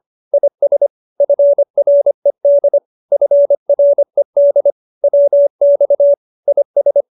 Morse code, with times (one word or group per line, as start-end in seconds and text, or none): 0.3-0.9 IS
1.2-2.8 FRED
3.1-4.7 FRED
5.0-6.1 WX
6.5-7.0 IS